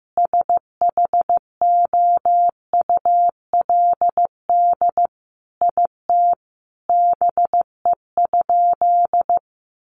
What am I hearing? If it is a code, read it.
Morse code, heard as a keyed tone, about 15 wpm, 700 Hz. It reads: SHOULD IT BE?
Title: Sentences from Top 300 Words 15wpm